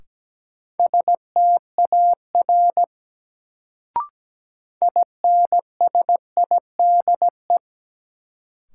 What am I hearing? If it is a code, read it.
Morse code, heard as a keyed tone, about 17 wpm, 700 Hz. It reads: STAR INSIDE
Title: Sentences from Top 1000 Words - Review Part 2 - New Word Drill - 17wpm